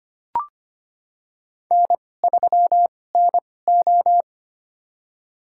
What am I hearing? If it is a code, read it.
Morse code, heard as a keyed tone, about 25 wpm, spaced as for 18 wpm, 700 Hz. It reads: N3NO
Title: USA Call Signs - 4 Characters - 2x Character Spacing - 25wpm